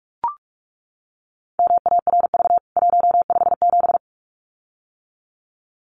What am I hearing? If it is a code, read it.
Morse code, heard as a keyed tone, about 44 wpm, 700 Hz. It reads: MARV 157
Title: CWT Exchange 45wpm